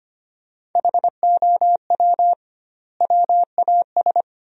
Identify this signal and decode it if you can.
Morse code, heard as a keyed tone, about 25 wpm, 700 Hz. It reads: HOW WAS
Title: 2x Word-Spacing - Sentences from Top 100 Words - 25wpm